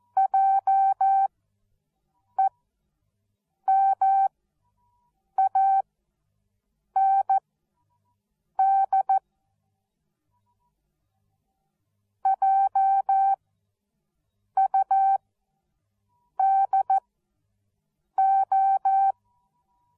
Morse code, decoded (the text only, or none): JEMAND JUDO